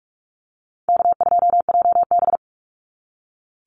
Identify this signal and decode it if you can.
Morse code, heard as a keyed tone, about 44 wpm, 700 Hz. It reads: K2JB
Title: NA SOTA Activator Callsigns 45wpm